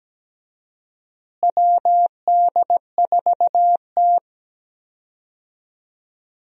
Morse code, heard as a keyed tone, about 17 wpm, 700 Hz. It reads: WD4T
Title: USA Call Signs - 4 Characters - Repeated 3x - 17wpm